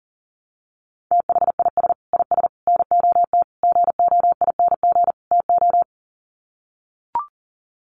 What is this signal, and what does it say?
Morse code, heard as a keyed tone, about 40 wpm, 700 Hz. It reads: THIS IS NOT GOING TO
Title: Sets of 5 Words Encore2 40wpm